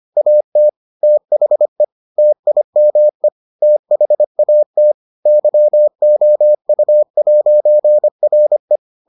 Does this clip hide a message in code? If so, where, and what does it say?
Yes: Morse code from 0.2 to 8.8 s, heard as a keyed tone, about 25 wpm, 600 Hz. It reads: AT THE TIME THAT YOU'RE